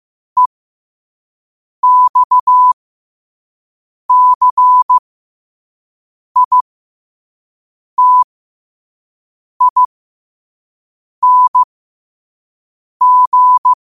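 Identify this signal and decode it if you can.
Morse code, heard as a keyed tone, about 15 wpm, spaced as for 5 wpm, 1000 Hz. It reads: EXCITING
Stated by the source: Morse code sent at 5 wpm.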